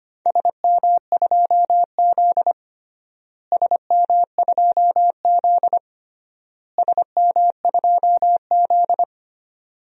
Morse code, heard as a keyed tone, about 25 wpm, 700 Hz. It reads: SM2Z SM2Z SM2Z
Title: DXCC-100 Call Signs - 4 Characters - Repeated 3x - 25wpm